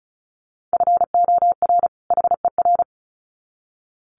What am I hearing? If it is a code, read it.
Morse code, heard as a keyed tone, about 34 wpm, 700 Hz. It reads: FOR HER
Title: Sets of 2 Words 35wpm